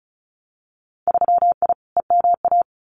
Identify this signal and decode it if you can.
Morse code, heard as a keyed tone, about 34 wpm, 700 Hz. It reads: 3I EMA
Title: Winter Field Day Exchanges Encore 35wpm